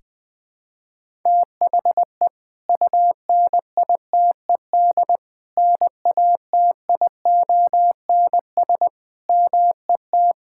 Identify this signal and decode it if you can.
Morse code, heard as a keyed tone, about 20 wpm, 700 Hz. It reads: THE UNITED NATIONS MET
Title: Sentences with 3-Word Phrases - 20wpm